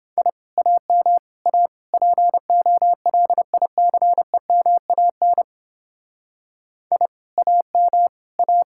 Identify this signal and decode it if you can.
Morse code, heard as a keyed tone, about 29 wpm, 700 Hz. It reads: I AM A POLICEMAN I AM A